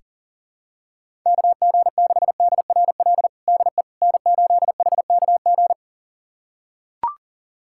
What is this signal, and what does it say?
Morse code, heard as a keyed tone, about 40 wpm, 700 Hz. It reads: KG6DRL DE N8HKG
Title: USA Callsign DE Callsign 40wpm